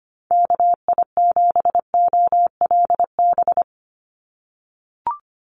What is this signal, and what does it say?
Morse code, heard as a keyed tone, about 25 wpm, 700 Hz. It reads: KI7OLB
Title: USA Call Signs 25wpm